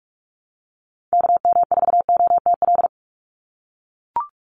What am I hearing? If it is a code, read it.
Morse code, heard as a keyed tone, about 44 wpm, 700 Hz. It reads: KM4OTL
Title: USA Call Signs 45wpm